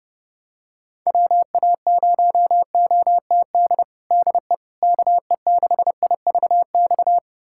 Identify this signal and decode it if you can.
Morse code, heard as a keyed tone, about 32 wpm, 700 Hz. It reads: WA0OTD DE KE6IVX
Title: USA Callsign DE Callsign 30wpm